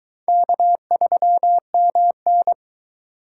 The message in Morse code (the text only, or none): K3MN